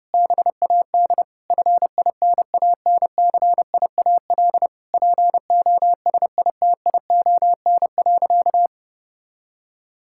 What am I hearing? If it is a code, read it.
Morse code, heard as a keyed tone, about 30 wpm, 700 Hz. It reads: BAD FINANCIAL POSITION.
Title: Sentences from Top 1000 Words - Review Part 5 - 30wpm